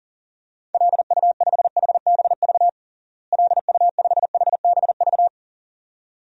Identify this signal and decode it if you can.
Morse code, heard as a keyed tone, about 40 wpm, 700 Hz. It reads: LU5HBV LU5HBV
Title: DXCC-100 Call Signs - 6 Characters - Repeated 2x - 40wpm